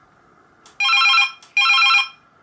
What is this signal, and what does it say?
There is no code.